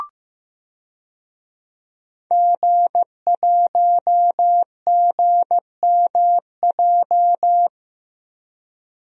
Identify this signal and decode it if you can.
Morse code, heard as a keyed tone, about 16 wpm, 700 Hz. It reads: G1GMJ
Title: DXCC-100 Call Signs - 5 Characters - Repeated 2x - 15wpm